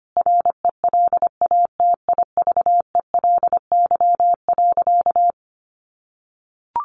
Morse code, heard as a keyed tone, about 25 wpm, 700 Hz. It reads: RELATIVELY.
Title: Sentences from Top 1100 Words 25wpm